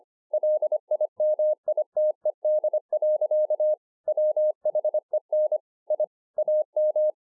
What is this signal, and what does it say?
Morse code, heard as a keyed tone, about 24 wpm, 610 Hz. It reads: LIMITED. WHEN I AM
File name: 2011-04-mp3 / THIS IS NOT MY SHACK 2 25WPM